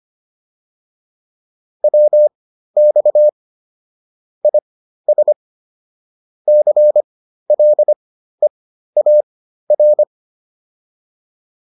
Morse code, heard as a keyed tone, about 25 wpm, spaced as for 13 wpm, 600 Hz. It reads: WX IS CLEAR
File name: PR402 13